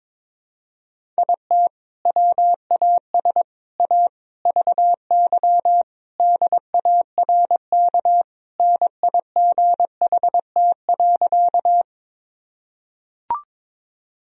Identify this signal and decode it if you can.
Morse code, heard as a keyed tone, about 22 wpm, 700 Hz. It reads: IT WAS A VY DARK NIGHT.
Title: Sentences from Top 500 Words and Abbreviations 22wpm